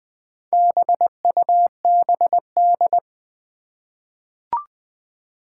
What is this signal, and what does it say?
Morse code, heard as a keyed tone, about 20 wpm, 700 Hz. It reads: BUBD
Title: Four Challenging Letter-Number Combinations 20wpm